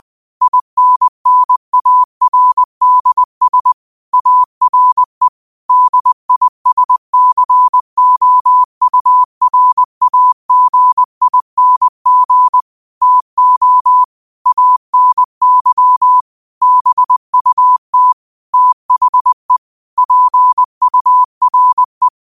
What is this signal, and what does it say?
Morse code, heard as a keyed tone, about 20 wpm, 1000 Hz. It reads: INNARDS ARE DISCOURAGING TO ANY BUT THE PURE
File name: qotdfiles / 20wpm